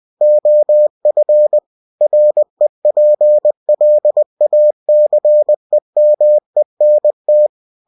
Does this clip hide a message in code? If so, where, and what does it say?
Yes: Morse code from 0.2 to 7.5 s, heard as a keyed tone, about 20 wpm, 600 Hz. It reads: OF REPLACEMENT